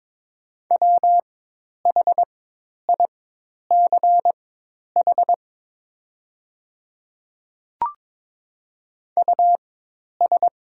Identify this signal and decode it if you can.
Morse code, heard as a keyed tone, about 22 wpm, spaced as for 10 wpm, 690 Hz. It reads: WHICH US
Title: 4x Character Spacing - Top 200 Words - 22wpm